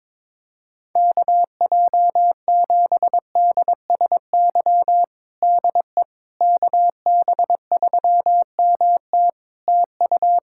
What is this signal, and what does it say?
Morse code, heard as a keyed tone, about 22 wpm, 700 Hz. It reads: KJ7DSY DE KB3MT TU
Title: K1USN SST QSO Practice 22wpm